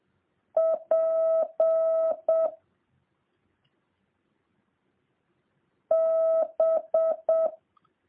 Morse code, heard as a keyed tone, about 7 wpm, 650 Hz. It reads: P B